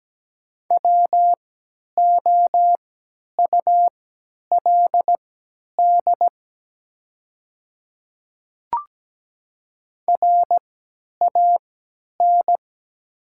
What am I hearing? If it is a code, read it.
Morse code, heard as a keyed tone, about 17 wpm, spaced as for 10 wpm, 690 Hz. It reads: WOULD RAN
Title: 3x Character Spacing - Top 500 Words - 17wpm